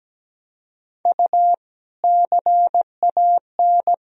Morse code, heard as a keyed tone, about 17 wpm, 700 Hz. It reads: U CAN
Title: Sentences from Top 400 Words and Abbreviations 17wpm